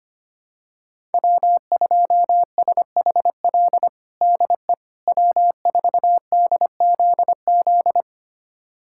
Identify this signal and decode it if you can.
Morse code, heard as a keyed tone, about 25 wpm, 700 Hz. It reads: W2SHL DE W4DZZ
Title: USA Callsign DE Callsign - 5 Characters - 25wpm